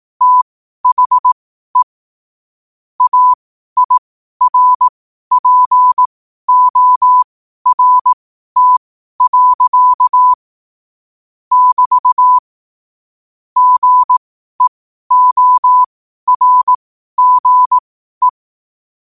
Morse code, heard as a keyed tone, about 18 wpm, spaced as for 12 wpm, 1000 Hz. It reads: THE AIRPORT. <BT> GEORGE
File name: qotdfiles / 12wpm